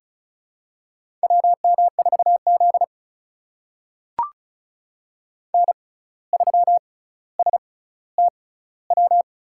Morse code, heard as a keyed tone, about 36 wpm, spaced as for 17 wpm, 700 Hz. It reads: WM4Z N3STW
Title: USA Call Signs - 6x Character Spacing - 35wpm